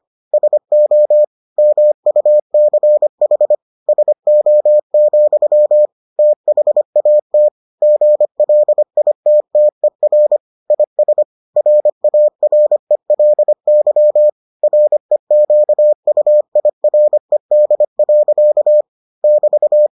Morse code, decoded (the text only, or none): SO MUCH SO, THAT GLITTER IS RARELY REQUIRED. <BT>